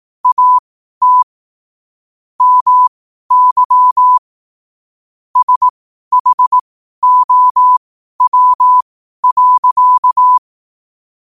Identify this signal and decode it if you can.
Morse code, heard as a keyed tone, about 18 wpm, spaced as for 12 wpm, 1000 Hz. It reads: AT MY SHOW.
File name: qotdfiles / 12wpm